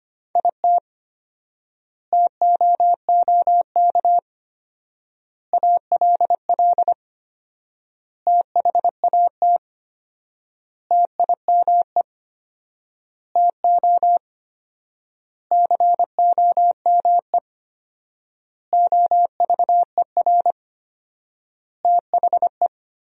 Morse code, heard as a keyed tone, about 25 wpm, 700 Hz. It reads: IT TOOK ALL THAT TIME TO COME OVER THE
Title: 4x Word-Spacing - Sentences from Top 300 Words - 25wpm